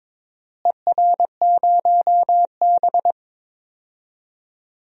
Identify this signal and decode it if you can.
Morse code, heard as a keyed tone, about 21 wpm, 700 Hz. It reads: ER0B
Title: DXCC-100 Call Signs - 4 Characters - Repeated 2x - 22wpm